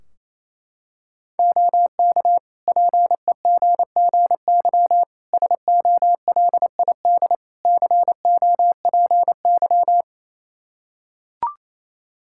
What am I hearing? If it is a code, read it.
Morse code, heard as a keyed tone, about 28 wpm, 700 Hz. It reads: OK PEGGY SOLID COPY
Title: Practice QSO - Exchange 3 - More Info - 28wpm